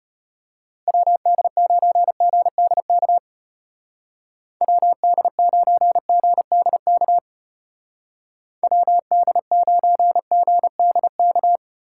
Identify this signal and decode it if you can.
Morse code, heard as a keyed tone, about 31 wpm, 700 Hz. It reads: WD9GDK WD9GDK WD9GDK
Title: Speed Racing - USA 2x3 Call Signs 25wpm